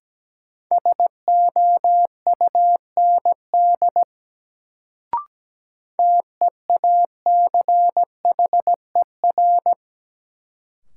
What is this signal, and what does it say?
Morse code, heard as a keyed tone, about 17 wpm, 700 Hz. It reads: SOUND TEACHER
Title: Sentences from Top 900 Words - New Word Drill - 17wpm